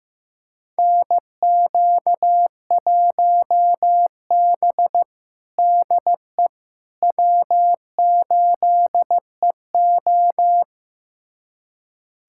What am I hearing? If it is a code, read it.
Morse code, heard as a keyed tone, about 15 wpm, 700 Hz. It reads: NQ1B DE W8EO
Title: USA Callsign DE Callsign - 4 Characters - 15wpm